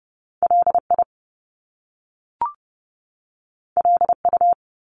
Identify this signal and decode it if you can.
Morse code, heard as a keyed tone, about 30 wpm, 700 Hz. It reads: LI LU
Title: Top Two Letter Combinations in Words 30wpm